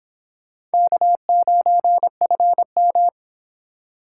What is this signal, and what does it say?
Morse code, heard as a keyed tone, about 26 wpm, 700 Hz. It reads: K9FM